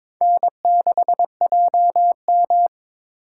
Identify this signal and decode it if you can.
Morse code, heard as a keyed tone, about 21 wpm, 700 Hz. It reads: N6JM
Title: USA Call Signs - 4 Characters - 2x Character Spacing - 22wpm